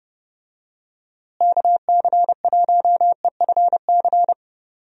Morse code, heard as a keyed tone, about 32 wpm, 700 Hz. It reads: KC1EFC
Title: USA Call Signs - 6 Characters - Repeated 3x - 30wpm